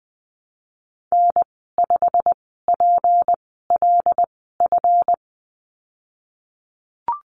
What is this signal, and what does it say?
Morse code, heard as a keyed tone, about 20 wpm, spaced as for 14 wpm, 700 Hz. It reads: N5PLF
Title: 161 Lesson 26g - Letter g - Call Signs 20wpm